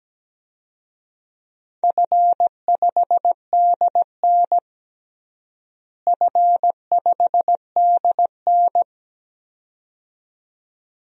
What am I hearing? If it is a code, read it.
Morse code, heard as a keyed tone, about 17 wpm, 700 Hz. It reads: F5DN F5DN
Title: DXCC-100 Call Signs - 4 Characters - Repeated 2x - 17wpm